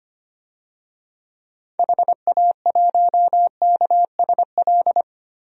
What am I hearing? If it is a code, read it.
Morse code, heard as a keyed tone, about 25 wpm, 700 Hz. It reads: HA1KSL
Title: DXCC-100 Call Signs - 6 Characters - 25wpm